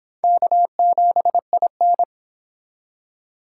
Morse code, heard as a keyed tone, about 25 wpm, 700 Hz. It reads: K7IN